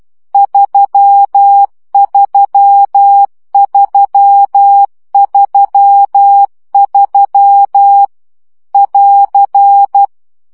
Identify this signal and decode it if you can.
Morse code, heard as a keyed tone, about 12 wpm, 800 Hz. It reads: 33333 <AR>
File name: CW gyakorlas 3 10 baud